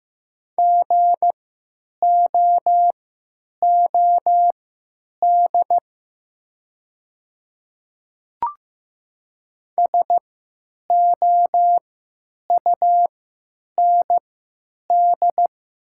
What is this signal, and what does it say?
Morse code, heard as a keyed tone, about 15 wpm, spaced as for 9 wpm, 700 Hz. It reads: GOOD SOUND
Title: 3x Character Spacing - Top 200 Words - 15wpm